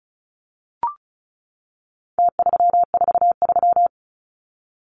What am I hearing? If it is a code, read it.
Morse code, heard as a keyed tone, about 34 wpm, 700 Hz. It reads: T343
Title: Numbers - Contest Style - 4-Digit 35wpm